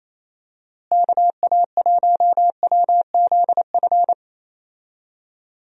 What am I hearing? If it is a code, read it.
Morse code, heard as a keyed tone, about 28 wpm, 700 Hz. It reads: KA1WZF
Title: USA Call Signs - 6 Characters - Repeated 3x - 28wpm